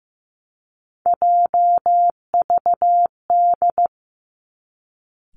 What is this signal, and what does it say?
Morse code, heard as a keyed tone, about 15 wpm, 700 Hz. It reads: JVD
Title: Three Challenging Letter-Number Combinations 15wpm